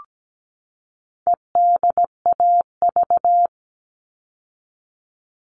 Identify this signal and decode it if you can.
Morse code, heard as a keyed tone, about 17 wpm, 700 Hz. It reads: EDAV
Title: All Cut Numbers - 4-Digit 17wpm